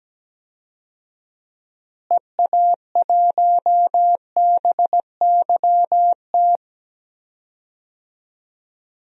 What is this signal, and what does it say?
Morse code, heard as a keyed tone, about 17 wpm, 700 Hz. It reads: EA1BYT
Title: DXCC-100 Call Signs - 6 Characters - Repeated 3x - 17wpm